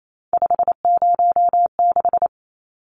Morse code, heard as a keyed tone, about 28 wpm, 700 Hz. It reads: 506
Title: Numbers - 3-Digit 28wpm